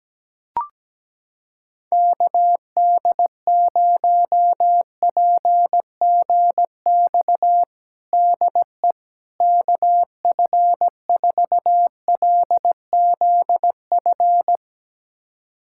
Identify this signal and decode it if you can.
Morse code, heard as a keyed tone, about 17 wpm, 700 Hz. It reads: KD0PGX DE KF4LZF
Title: USA Callsign DE Callsign 17wpm